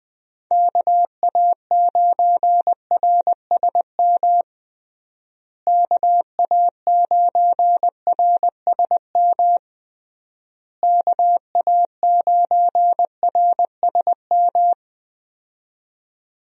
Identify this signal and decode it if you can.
Morse code, heard as a keyed tone, about 21 wpm, 700 Hz. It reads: KA9RSM KA9RSM KA9RSM
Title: USA Call Signs - 6 Characters - Repeated 3x - 20wpm